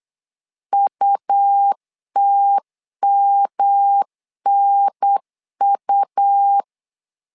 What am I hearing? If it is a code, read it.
Morse code, heard as a keyed tone, about 8 wpm, 800 Hz. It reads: UTMNU
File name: LEZIONI / 3 MWRO / 3 M 8wpm 2300